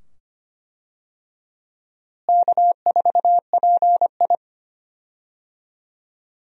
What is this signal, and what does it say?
Morse code, heard as a keyed tone, about 25 wpm, 700 Hz. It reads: K4PI
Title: USA Call Signs - 4 Characters - Repeated 2x - 25wpm